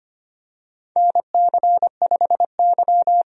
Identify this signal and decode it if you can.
Morse code, heard as a keyed tone, about 25 wpm, 700 Hz. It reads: NC5Y
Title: USA Call Signs - 4 Characters - 25wpm